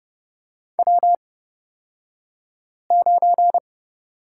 Morse code, heard as a keyed tone, about 30 wpm, spaced as for 5 wpm, 700 Hz. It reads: W9
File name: Lesson 38d - Letter q - Two Characters 30wpm